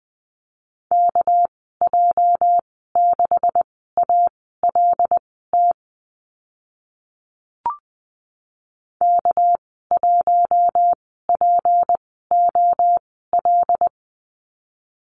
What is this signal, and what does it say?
Morse code, heard as a keyed tone, about 20 wpm, spaced as for 15 wpm, 700 Hz. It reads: KJ6ALT K1POL
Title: 231 Lesson 37g - Letter x - Call Signs 20wpm